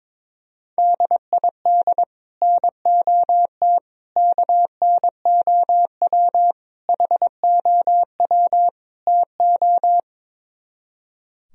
Morse code, heard as a keyed tone, about 23 wpm, 700 Hz. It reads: DID NOT KNOW HOW TO
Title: Speed-Racing - Sets of 5 Words 22wpm